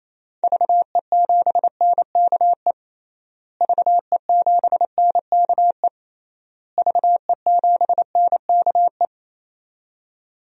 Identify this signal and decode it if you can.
Morse code, heard as a keyed tone, about 28 wpm, 700 Hz. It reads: VE7NKE VE7NKE VE7NKE
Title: DXCC-100 Call Signs - 6 Characters - Repeated 3x - 28wpm